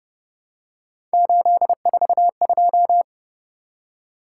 Morse code, heard as a keyed tone, about 31 wpm, 700 Hz. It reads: 842